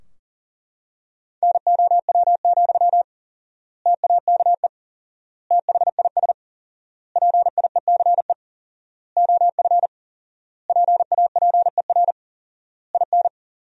Morse code, heard as a keyed tone, about 37 wpm, 700 Hz. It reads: NOW, TAKE THIS PIECE OF PAPER IN
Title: 4x Word-Spacing - Sentences from Top 400 Words - 40wpm